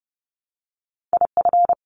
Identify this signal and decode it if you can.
Morse code, heard as a keyed tone, about 30 wpm, 700 Hz. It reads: IF
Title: Sentences from Top 500 Words 30wpm